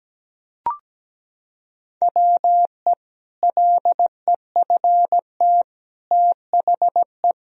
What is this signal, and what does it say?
Morse code, heard as a keyed tone, about 17 wpm, 700 Hz. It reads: WE LEFT THE
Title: Sentences from Top 1000 Words - Review Part 1 - 17wpm